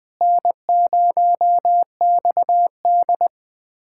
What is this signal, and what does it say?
Morse code, heard as a keyed tone, about 21 wpm, 700 Hz. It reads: N0XD
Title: USA Callsign DE Callsign - 4 Characters - 20wpm